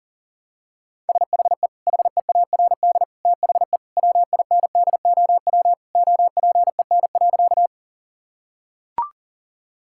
Morse code, heard as a keyed tone, about 37 wpm, 700 Hz. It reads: SHE HEARD THE WINDOW OPEN.
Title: Sentences from Top 1600 Words 40wpm